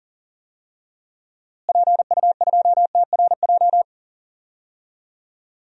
Morse code, heard as a keyed tone, about 40 wpm, 700 Hz. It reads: PU2TRJ
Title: DXCC-100 Call Signs - 6 Characters - Repeated 3x - 40wpm